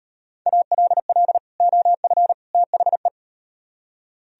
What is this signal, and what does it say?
Morse code, heard as a keyed tone, about 38 wpm, 700 Hz. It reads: ALL OF THE